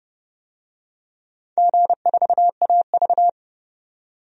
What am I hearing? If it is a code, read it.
Morse code, heard as a keyed tone, about 29 wpm, 700 Hz. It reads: G4AV
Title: DXCC-100 Call Signs - 4 Characters - 30wpm